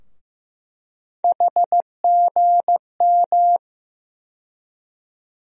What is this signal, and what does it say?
Morse code, heard as a keyed tone, about 15 wpm, 700 Hz. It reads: HGM
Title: USA Call Sign Postfixes - 3 Letters 15wpm